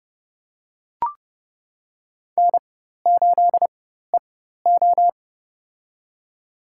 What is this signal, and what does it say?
Morse code, heard as a keyed tone, about 30 wpm, spaced as for 14 wpm, 700 Hz. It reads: N8EO
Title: USA Call Signs - 4 Characters - 4x Character Spacing - 30wpm